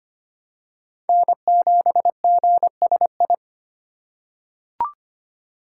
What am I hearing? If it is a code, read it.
Morse code, heard as a keyed tone, about 25 wpm, 700 Hz. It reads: N7GSI
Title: USA Call Signs - 5 Characters - 4x Character Spacing - 25wpm